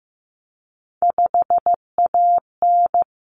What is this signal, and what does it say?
Morse code, heard as a keyed tone, about 15 wpm, 700 Hz. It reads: 5AN
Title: Common Cut Numbers - 3-Digit 15wpm